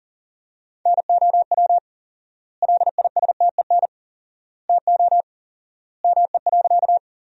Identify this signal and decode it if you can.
Morse code, heard as a keyed tone, about 40 wpm, 700 Hz. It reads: NOW LISTEN TO ME.
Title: 4x Word-Spacing - Sentences from Top 400 Words - 40wpm